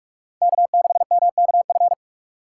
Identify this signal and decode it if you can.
Morse code, heard as a keyed tone, about 46 wpm, 700 Hz. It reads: K6MKF
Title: K1USN SST QSO Practice - 45wpm